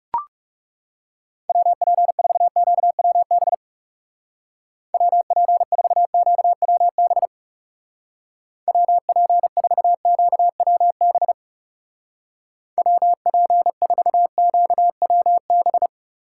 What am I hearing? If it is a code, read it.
Morse code, heard as a keyed tone, about 37 wpm, 700 Hz. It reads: WP4QWB WP4QWB WP4QWB WP4QWB
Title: Speed Racing - USA 2x3 Call Signs 30wpm